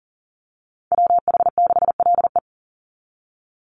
Morse code, heard as a keyed tone, about 40 wpm, 700 Hz. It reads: WH6LE
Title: NA SOTA Chaser Callsigns 40wpm